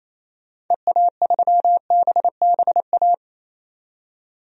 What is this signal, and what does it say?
Morse code, heard as a keyed tone, about 27 wpm, 700 Hz. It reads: EA3BBA
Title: DXCC-100 Call Signs - 6 Characters - 28wpm